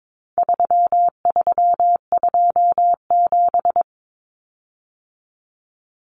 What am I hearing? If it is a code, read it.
Morse code, heard as a keyed tone, about 22 wpm, 700 Hz. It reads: 3327
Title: Common Cut Numbers - 4-Digit 22wpm